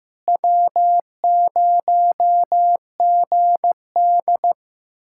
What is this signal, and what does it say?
Morse code, heard as a keyed tone, about 15 wpm, 700 Hz. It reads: W0GD
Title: USA Callsign DE Callsign - 4 Characters - 15wpm